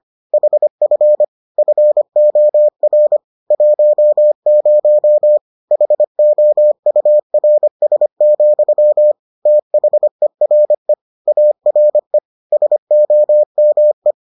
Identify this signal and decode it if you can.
Morse code, heard as a keyed tone, about 25 wpm, 600 Hz. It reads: HF FOR 10 HOURS, THERE ARE SOME